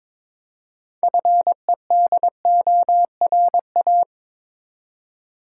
Morse code, heard as a keyed tone, about 23 wpm, 690 Hz. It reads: FEDORA
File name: Difficult Words with Hints - 22wpm